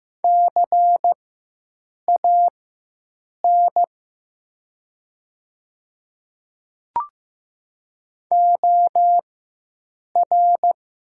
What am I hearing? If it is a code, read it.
Morse code, heard as a keyed tone, about 15 wpm, spaced as for 7 wpm, 700 Hz. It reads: CAN OR
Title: 4x Character Spacing - Top 200 Words - 15wpm